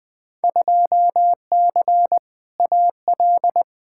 Morse code, heard as a keyed tone, about 20 wpm, 700 Hz. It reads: 2C AL
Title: Field Day Exchanges Encore - 20wpm